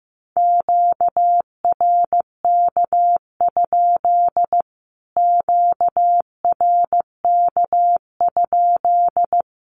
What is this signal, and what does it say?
Morse code, heard as a keyed tone, about 15 wpm, 700 Hz. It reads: QRK? QRK?
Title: Q-Codes Repeated 3x 15wpm